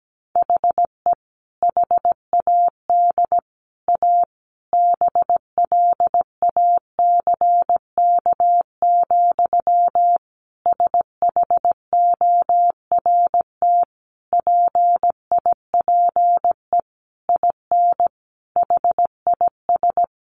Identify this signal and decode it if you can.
Morse code, heard as a keyed tone, about 17 wpm, 700 Hz. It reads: HE HAD A BLACK, SHORT PIPE IN HIS
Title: Sentences from Top 1500 Words 17wpm